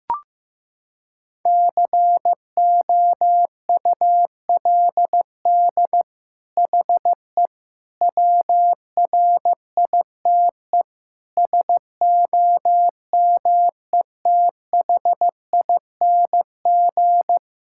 Morse code, heard as a keyed tone, about 15 wpm, 700 Hz. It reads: COULD HE WRITE SOMETHING
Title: Sentences from Top 1000 Words - Review Part 6 - 15wpm